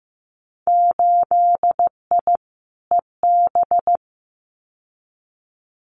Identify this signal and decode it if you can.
Morse code, heard as a keyed tone, about 15 wpm, 700 Hz. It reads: 8I EB
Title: Winter Field Day Exchanges Encore 15wpm